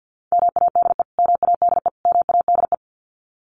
Morse code, heard as a keyed tone, about 50 wpm, 700 Hz. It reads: MADE MADE MADE
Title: Top 200 Words Repeated 50wpm